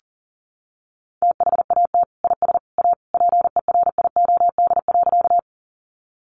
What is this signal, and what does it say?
Morse code, heard as a keyed tone, about 40 wpm, 700 Hz. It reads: THAT IS A PERIOD.
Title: Sentences from Top 800 Words 40wpm